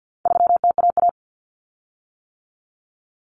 Morse code, heard as a keyed tone, about 50 wpm, 700 Hz. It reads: 3TAA
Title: Common Cut Numbers - 4-Digit 50wpm